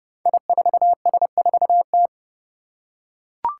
Morse code, heard as a keyed tone, about 29 wpm, 700 Hz. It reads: I4S4T
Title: Five Challenging Letter-Number Combinations 30wpm